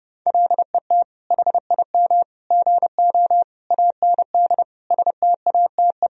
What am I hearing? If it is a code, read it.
Morse code, heard as a keyed tone, about 30 wpm, 700 Hz. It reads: LET HIM GO AND STATE
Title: Sentences from Top 1000 Words - Review Part 2 - 30wpm